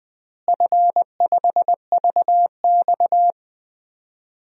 Morse code, heard as a keyed tone, about 20 wpm, 700 Hz. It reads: F5VX
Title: DXCC-100 Call Signs - 4 Characters - Repeated 2x - 20wpm